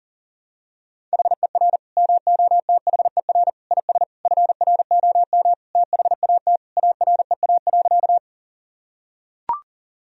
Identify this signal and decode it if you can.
Morse code, heard as a keyed tone, about 37 wpm, 700 Hz. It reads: HER MOTHER IS FROM THAT AREA.
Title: Sentences from Top 1000 Words - Review Part 2 - 40wpm